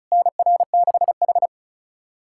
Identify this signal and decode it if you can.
Morse code, heard as a keyed tone, about 36 wpm, 690 Hz. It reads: NR6H
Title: SOTA QSO Practice 35wpm